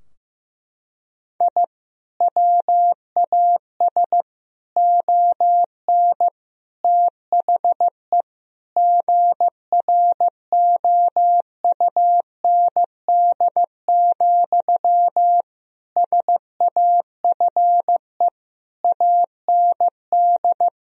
Morse code, heard as a keyed tone, about 15 wpm, 700 Hz. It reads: I WAS ON THE GROUND, SAFE AND
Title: Sentences from Top 1000 Words - Review Part 2 - 15wpm